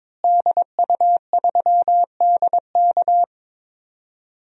Morse code, heard as a keyed tone, about 22 wpm, 700 Hz. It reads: DU3DK
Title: DXCC-100 Call Signs - 5 Characters - Repeated 3x - 22wpm